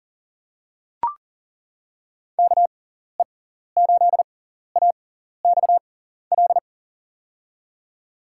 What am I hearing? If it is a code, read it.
Morse code, heard as a keyed tone, about 39 wpm, spaced as for 14 wpm, 700 Hz. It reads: KE8AXL
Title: USA Call Signs - 6x Character Spacing - 40wpm